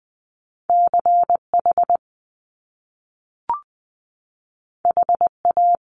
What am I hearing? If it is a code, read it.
Morse code, heard as a keyed tone, about 20 wpm, 700 Hz. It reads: CH HA
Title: Top Two Letter Combinations in Words 20wpm